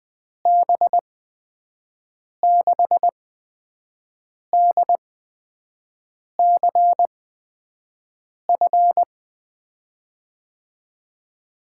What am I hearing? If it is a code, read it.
Morse code, heard as a keyed tone, about 20 wpm, spaced as for 5 wpm, 700 Hz. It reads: B6DCF
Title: USA Call Signs - 6 Characters - 8x Character Spacing - 20wpm